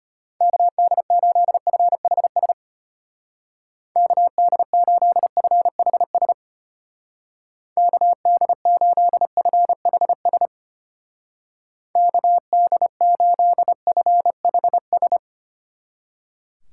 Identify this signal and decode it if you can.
Morse code, heard as a keyed tone, about 32 wpm, 700 Hz. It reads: KD8FHS KD8FHS KD8FHS KD8FHS